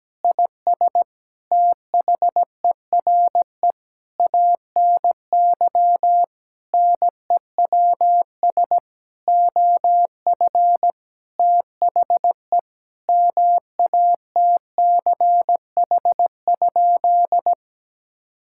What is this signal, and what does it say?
Morse code, heard as a keyed tone, about 17 wpm, 700 Hz. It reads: IS THERE ANY NEWS OF THE MATCH?
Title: Sentences from Top 1600 Words 17wpm